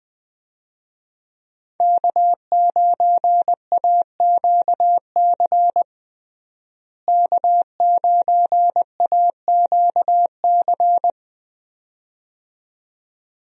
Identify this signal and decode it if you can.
Morse code, heard as a keyed tone, about 20 wpm, 700 Hz. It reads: K9AQC K9AQC
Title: USA Call Signs - 5 Characters - Repeated 2x - 20wpm